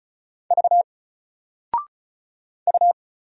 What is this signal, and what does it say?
Morse code, heard as a keyed tone, about 36 wpm, 700 Hz. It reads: V U
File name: Single-Letter-Number-35wpm